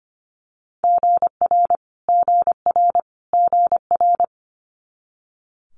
Morse code, heard as a keyed tone, about 25 wpm, 700 Hz. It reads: GR GR GR
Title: Multi-letter Phonemes Repeated 3x 25wpm